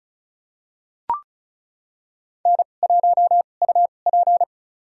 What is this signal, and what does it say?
Morse code, heard as a keyed tone, about 36 wpm, spaced as for 25 wpm, 700 Hz. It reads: N1UP
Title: USA Call Signs - 4 Characters - 2x Character Spacing - 35wpm